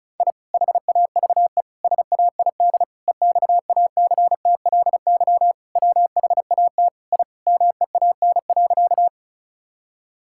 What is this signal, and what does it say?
Morse code, heard as a keyed tone, about 36 wpm, 700 Hz. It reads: I HAVE SAID EXACTLY WHAT I MEAN.
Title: Sentences from Top 1000 Words - Review Part 6 - 35wpm